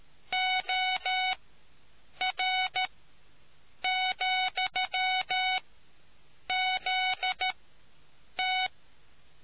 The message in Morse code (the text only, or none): OR,ZT